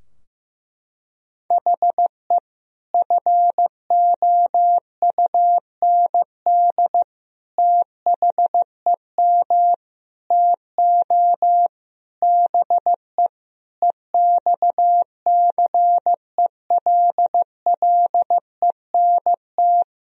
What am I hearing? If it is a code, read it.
Morse code, heard as a keyed tone, about 15 wpm, 700 Hz. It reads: HE FOUND THEM TO BE EXCELLENT
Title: Sentences from Top 1000 Words - Review Part 3 - 15wpm